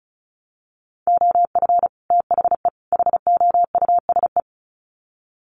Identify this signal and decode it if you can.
Morse code, heard as a keyed tone, about 34 wpm, 700 Hz. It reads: OF THE HOUSE
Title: Sets of 3 Words 35wpm